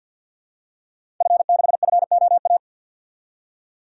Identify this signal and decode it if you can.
Morse code, heard as a keyed tone, about 50 wpm, 700 Hz. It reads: F6FOA
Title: Code Groups - 5 Letter:Numbers - 50wpm